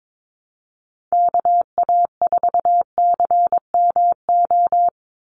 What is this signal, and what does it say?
Morse code, heard as a keyed tone, about 22 wpm, 700 Hz. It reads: KA4CMO
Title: USA Call Signs 22wpm